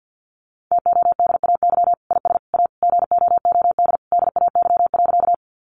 Morse code, heard as a keyed tone, about 50 wpm, 700 Hz. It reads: TODAY IS A GOOD DAY.
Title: Sentences from Top 600 Words 50wpm